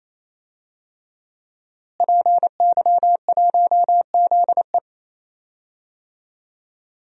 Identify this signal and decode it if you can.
Morse code, heard as a keyed tone, about 28 wpm, 700 Hz. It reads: PY1ZE
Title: DXCC-100 Call Signs - 5 Characters - Repeated 3x - 28wpm